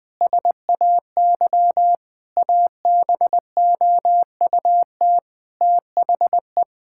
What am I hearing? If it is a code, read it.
Morse code, heard as a keyed tone, about 21 wpm, 700 Hz. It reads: SAY ABOUT THE
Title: Sentences from Top 1000 Words - Review Part 5 - 20wpm